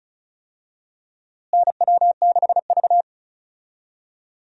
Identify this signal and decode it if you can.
Morse code, heard as a keyed tone, about 36 wpm, 700 Hz. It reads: NW6V
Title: USA Call Signs - 4 Characters - Repeated 3x - 35wpm